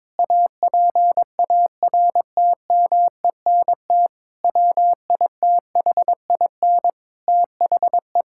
Morse code, heard as a keyed tone, about 22 wpm, 700 Hz. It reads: APARTMENT WITHIN THE
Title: Sentences from Top 1000 Words - Review Part 6 - 22wpm